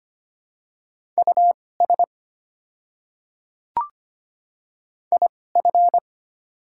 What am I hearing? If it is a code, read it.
Morse code, heard as a keyed tone, about 25 wpm, spaced as for 18 wpm, 700 Hz. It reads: US IF
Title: 2x Character Spacing - Top 500 Words - 25wpm